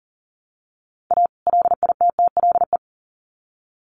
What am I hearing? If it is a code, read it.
Morse code, heard as a keyed tone, about 40 wpm, 700 Hz. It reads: A LITTLE
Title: Sets of 2 Words 40wpm